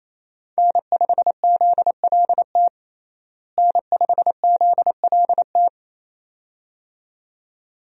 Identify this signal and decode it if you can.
Morse code, heard as a keyed tone, about 28 wpm, 700 Hz. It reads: N5ZLT N5ZLT
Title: USA Call Signs - 5 Characters - Repeated 2x - 28wpm